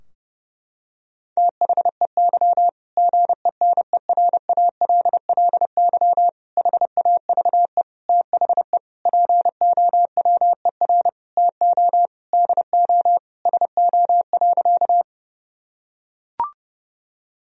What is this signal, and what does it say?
Morse code, heard as a keyed tone, about 32 wpm, 700 Hz. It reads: THEY GENERALLY HAVE THE POWER TO DO SO.
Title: Sentences from Top 1000 Words - Review Part 4 - 30wpm